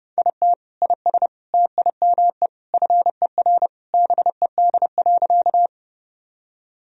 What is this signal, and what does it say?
Morse code, heard as a keyed tone, about 30 wpm, 700 Hz. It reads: IT IS TIME FER BED.
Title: Sentences from Top 500 Words and Abbreviations 30wpm